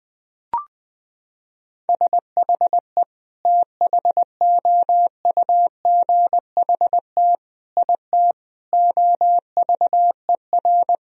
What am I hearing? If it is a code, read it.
Morse code, heard as a keyed tone, about 21 wpm, 700 Hz. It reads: SHE THOUGHT IT OVER
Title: Sentences from Top 1000 Words - Review Part 1 - 20wpm